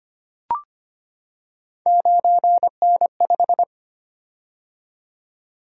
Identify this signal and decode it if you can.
Morse code, heard as a keyed tone, about 25 wpm, 700 Hz. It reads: 9N5
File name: Three Letter-Number Combinations - 25wpm